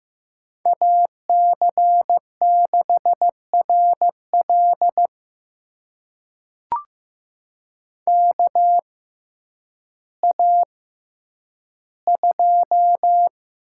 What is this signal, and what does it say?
Morse code, heard as a keyed tone, about 15 wpm, spaced as for 9 wpm, 700 Hz. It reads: AC6RL KA2
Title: USA Call Signs - 6x Character Spacing - 15wpm